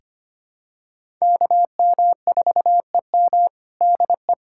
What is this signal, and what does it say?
Morse code, heard as a keyed tone, about 25 wpm, 700 Hz. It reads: KM4EM DE
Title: Practice QSO - Exchange 4 - Ending - 25wpm